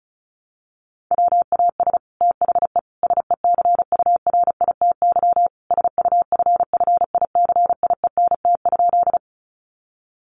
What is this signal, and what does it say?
Morse code, heard as a keyed tone, about 34 wpm, 700 Hz. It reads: WAS THE SECURITY SUFFICIENT?
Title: Sentences from Top 1400 Words 35wpm